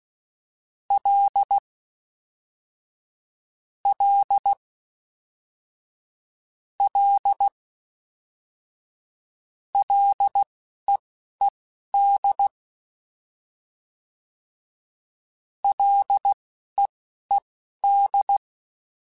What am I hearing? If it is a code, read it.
Morse code, heard as a keyed tone, about 16 wpm, spaced as for 11 wpm, 800 Hz. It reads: L L L LEED LEED